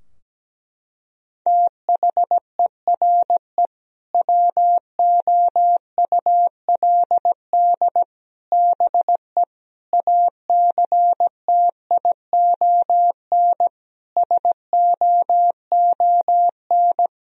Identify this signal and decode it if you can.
Morse code, heard as a keyed tone, about 17 wpm, 700 Hz. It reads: THERE WOULD BE ACTION SOON
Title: Sentences from Top 1000 Words - Review Part 4 - 17wpm